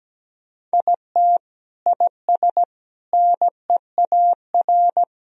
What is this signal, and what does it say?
Morse code, heard as a keyed tone, about 17 wpm, 700 Hz. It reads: IT IS NEAR
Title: Sentences from Top 1000 Words - Review Part 1 - 17wpm